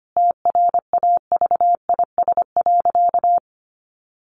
Morse code, heard as a keyed tone, about 25 wpm, 700 Hz. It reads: TRAVIS.
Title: A Sound Of Thunder 25wpm